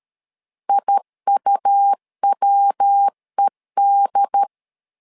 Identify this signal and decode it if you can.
Morse code, heard as a keyed tone, about 13 wpm, 800 Hz. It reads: IUWED
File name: LEZIONI / 3 MWRO / 3 MW 12wpm 1633